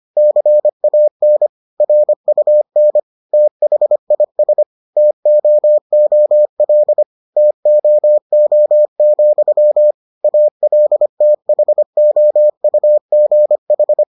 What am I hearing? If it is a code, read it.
Morse code, heard as a keyed tone, about 25 wpm, 600 Hz. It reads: CAN RUN THIS TOOL TOO, ALTHOUGH